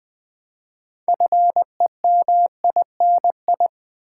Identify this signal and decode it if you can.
Morse code, heard as a keyed tone, about 20 wpm, 700 Hz. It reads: FEMINI
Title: Build A Word - 3 Letters At A Time - 8-Letter Words - 20wpm